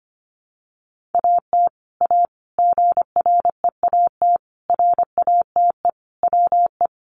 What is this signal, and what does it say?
Morse code, heard as a keyed tone, about 25 wpm, 700 Hz. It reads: AT A GREAT RATE WE
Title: Sentences from Top 1200 Words 25wpm